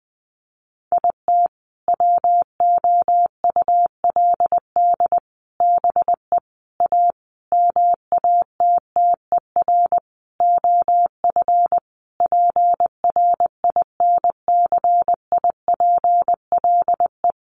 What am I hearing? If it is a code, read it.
Morse code, heard as a keyed tone, about 20 wpm, 700 Hz. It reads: IT WOULD BE A MATTER OF PRINCIPLE
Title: Sentences from Top 1400 Words 20wpm